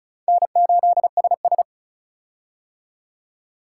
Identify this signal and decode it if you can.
Morse code, heard as a keyed tone, about 36 wpm, 700 Hz. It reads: N8SS
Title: K1USN SST QSO Practice - 35wpm